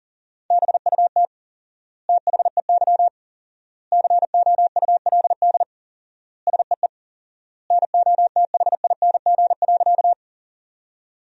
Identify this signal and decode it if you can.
Morse code, heard as a keyed tone, about 40 wpm, 700 Hz. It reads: BUT THEY COULD SEE NOTHING.
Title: 4x Word-Spacing - Sentences from Top 500 Words - 40wpm